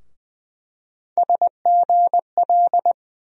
Morse code, heard as a keyed tone, about 20 wpm, 700 Hz. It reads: SGL
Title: USA Call Sign Postfixes - 3 Letters 20wpm